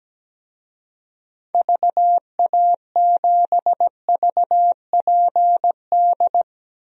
Morse code, heard as a keyed tone, about 17 wpm, 700 Hz. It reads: VA7VPD
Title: DXCC-100 Call Signs - 6 Characters - Repeated 3x - 17wpm